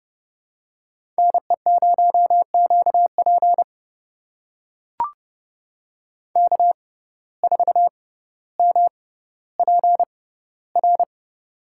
Morse code, heard as a keyed tone, about 30 wpm, spaced as for 15 wpm, 700 Hz. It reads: NE0QP K4MPR
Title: USA Call Signs - 5 Characters - 6x Character Spacing - 30wpm